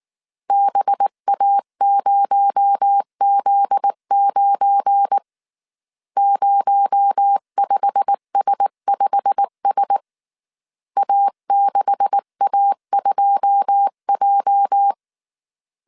Morse code, heard as a keyed tone, about 19 wpm, 800 Hz. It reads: BA0Z9 05S5S A6A2J
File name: CW / DET / 1 9 1500C S B 18wpm 2551